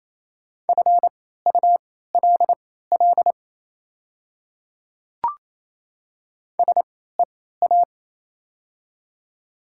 Morse code, heard as a keyed tone, about 28 wpm, spaced as for 16 wpm, 700 Hz. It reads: FULL SEA
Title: 3x Character Spacing - Top 500 Words - 28wpm